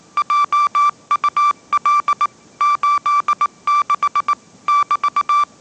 Morse code, heard as a keyed tone, about 18 wpm, 1170 Hz. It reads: JUL 86 <BT>
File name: 20wpm